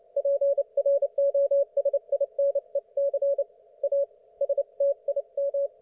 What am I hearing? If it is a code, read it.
Morse code, heard as a keyed tone, about 29 wpm, 560 Hz. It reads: PROSINEC A STIM